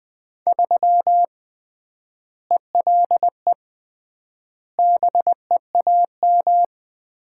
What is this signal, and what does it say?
Morse code, heard as a keyed tone, about 20 wpm, 700 Hz. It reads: 3 ELE BEAM
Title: Common QSO Phrases - 3x Word-Spacing - 20wpm